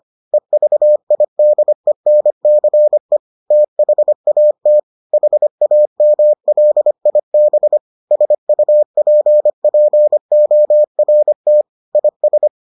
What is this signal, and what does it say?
Morse code, heard as a keyed tone, about 25 wpm, 600 Hz. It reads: EVIDENCE THAT HAMLIB SUPPORT IS